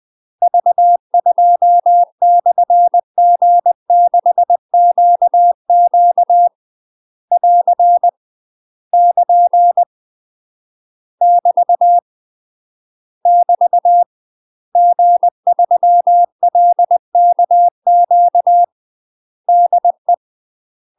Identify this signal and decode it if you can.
Morse code, heard as a keyed tone, about 20 wpm, 700 Hz. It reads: V2/G6QQ <AR> <KN> <BT> <BT> G3LKQ DE
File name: UK MASTER QSOs TRAINING / 20WPM / UK MASTER QSOs 0035